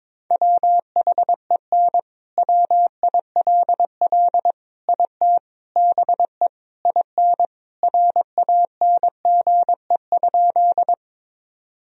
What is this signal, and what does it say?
Morse code, heard as a keyed tone, about 22 wpm, 700 Hz. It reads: WHEN WILL IT BE IN RANGE?
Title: Sentences from Top 1000 Words - Review Part 4 - 22wpm